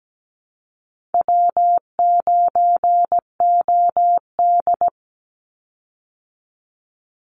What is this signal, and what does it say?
Morse code, heard as a keyed tone, about 17 wpm, 700 Hz. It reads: W9OD
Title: Call Signs Repeated 2x - 17wpm